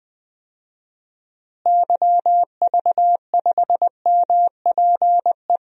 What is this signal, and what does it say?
Morse code, heard as a keyed tone, about 19 wpm, 700 Hz. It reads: YV5MPE
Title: DXCC-100 Call Signs - 6 Characters - Repeated 3x - 20wpm